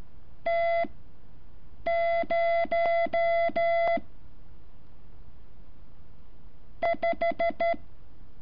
Morse code, heard as a keyed tone, about 13 wpm, spaced as for 6 wpm, 680 Hz. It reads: T0 5
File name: curso / ejercicio 14